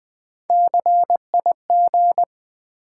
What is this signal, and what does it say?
Morse code, heard as a keyed tone, about 20 wpm, 700 Hz. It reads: CIG
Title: Build A Word - 3 Letters At A Time - 9-Letter Words - 20wpm